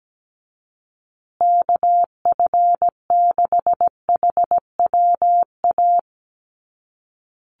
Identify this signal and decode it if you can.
Morse code, heard as a keyed tone, about 17 wpm, 700 Hz. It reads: KF6HWA
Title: USA Call Signs Repeated 2x 17wpm